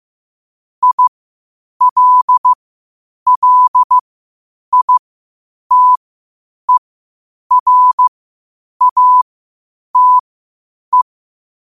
Morse code, heard as a keyed tone, about 15 wpm, spaced as for 8 wpm, 1000 Hz. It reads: ILLITERATE
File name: qotdfiles / 8wpm